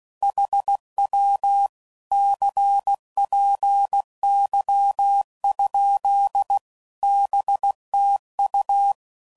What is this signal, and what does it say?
Morse code, heard as a keyed tone, about 16 wpm, 800 Hz. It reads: HW CPY? BTU
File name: CW Code Groups @15WPM / QSOs / WA7QOX